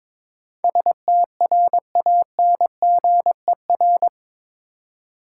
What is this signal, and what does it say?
Morse code, heard as a keyed tone, about 22 wpm, 700 Hz. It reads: STRANGER
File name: practice / 22wpm / Eight Letter Words - 22wpm